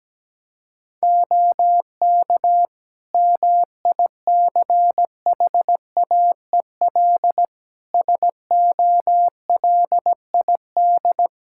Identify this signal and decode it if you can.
Morse code, heard as a keyed tone, about 17 wpm, 700 Hz. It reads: OK MICHAEL SOLID
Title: K1USN SST QSO Practice 17wpm